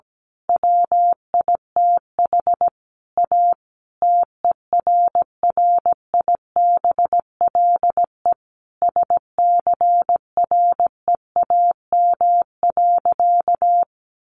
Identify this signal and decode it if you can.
Morse code, heard as a keyed tone, about 17 wpm, 700 Hz. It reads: WITH A TERRIBLE SCREAM.
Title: A Sound Of Thunder 17wpm